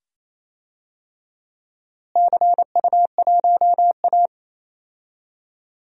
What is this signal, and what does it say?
Morse code, heard as a keyed tone, about 27 wpm, 700 Hz. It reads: CU1A
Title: DXCC-100 Call Signs - 4 Characters - Repeated 2x - 28wpm